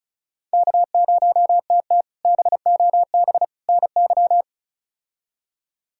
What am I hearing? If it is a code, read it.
Morse code, heard as a keyed tone, about 36 wpm, 700 Hz. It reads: K0TT BOB NY
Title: K1USN SST QSO Practice - 35wpm